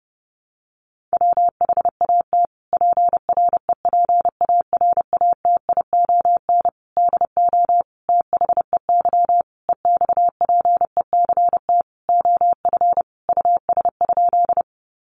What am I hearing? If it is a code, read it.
Morse code, heard as a keyed tone, about 30 wpm, 700 Hz. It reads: WHAT PREPARATION DO THEY EXPECT OF US?
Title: Sentences from Top 1400 Words 30wpm